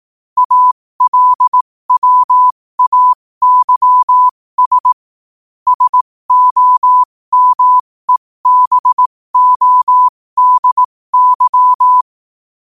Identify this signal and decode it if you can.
Morse code, heard as a keyed tone, about 18 wpm, spaced as for 15 wpm, 1000 Hz. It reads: ALWAYS SOMEBODY
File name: qotdfiles / 15wpm